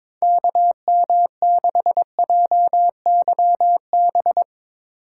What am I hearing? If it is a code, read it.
Morse code, heard as a keyed tone, about 24 wpm, 700 Hz. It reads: KM6JYB
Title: Speed Racing - USA 2x3 Call Signs 22wpm